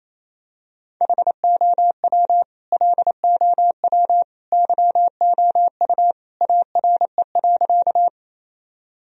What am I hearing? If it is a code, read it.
Morse code, heard as a keyed tone, about 28 wpm, 700 Hz. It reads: HOW LOW YOU ARE.
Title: Sentences from Top 1000 Words - Review Part 1 - 28wpm